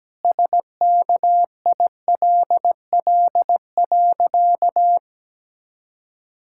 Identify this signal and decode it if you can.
Morse code, heard as a keyed tone, about 17 wpm, 700 Hz. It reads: SKILL.
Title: Pangram Challenge v2 - 17wpm